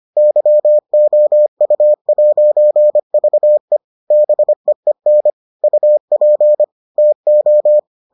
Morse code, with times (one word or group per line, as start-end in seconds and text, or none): 0.2-3.8 YOU'VE
4.1-5.3 BEEN
5.6-6.6 UP
7.0-7.8 TO